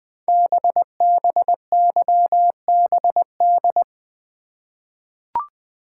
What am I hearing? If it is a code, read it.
Morse code, heard as a keyed tone, about 20 wpm, 700 Hz. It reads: BBYBD
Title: Code Groups - 5 Letters - 20wpm